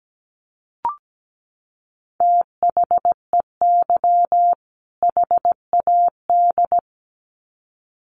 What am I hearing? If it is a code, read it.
Morse code, heard as a keyed tone, about 17 wpm, 700 Hz. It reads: THEY HAD
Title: Sets of 2 Words 17wpm